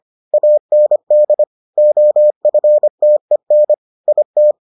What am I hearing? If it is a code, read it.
Morse code, heard as a keyed tone, about 25 wpm, 600 Hz. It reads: AND OFTEN IT